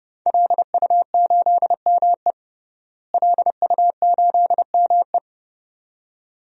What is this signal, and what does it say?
Morse code, heard as a keyed tone, about 32 wpm, 700 Hz. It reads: LU8ME LU8ME
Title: DXCC-100 Call Signs - 5 Characters - Repeated 2x - 30wpm